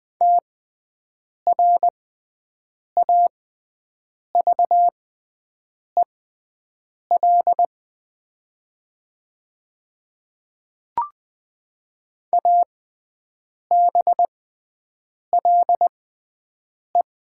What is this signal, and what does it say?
Morse code, heard as a keyed tone, about 20 wpm, spaced as for 7 wpm, 700 Hz. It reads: TRAVEL ABLE
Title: 6x Character Spacing - Top 500 Words - 20wpm